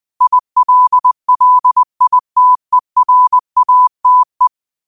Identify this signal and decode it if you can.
Morse code, heard as a keyed tone, about 20 wpm, 1000 Hz. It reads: ILLITERATE
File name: qotdfiles / 20wpm